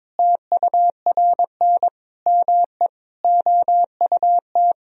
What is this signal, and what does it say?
Morse code, heard as a keyed tone, about 22 wpm, 700 Hz. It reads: TURN ME OUT
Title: Sentences from Top 1000 Words - Review Part 4 - 22wpm